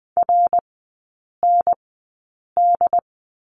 Morse code, heard as a keyed tone, about 20 wpm, spaced as for 8 wpm, 700 Hz. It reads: RND
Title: 067 Lesson 12e - Letter d - Three Characters 20wpm